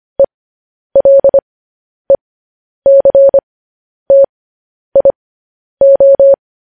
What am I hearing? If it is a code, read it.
Morse code, heard as a keyed tone, about 25 wpm, spaced as for 10 wpm, 550 Hz. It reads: ELECTIO
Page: news-headlines-in-morse-code-at-10-wpm